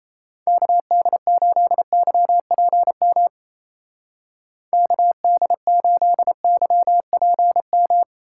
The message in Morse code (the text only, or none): KD8YPM KD8YPM